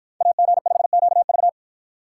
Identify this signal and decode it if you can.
Morse code, heard as a keyed tone, about 53 wpm, 700 Hz. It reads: AG5QV